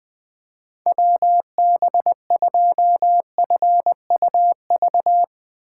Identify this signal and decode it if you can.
Morse code, heard as a keyed tone, about 20 wpm, 700 Hz. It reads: WB2FUV
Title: 2023 SOTA Chaser Call Signs - 20wpm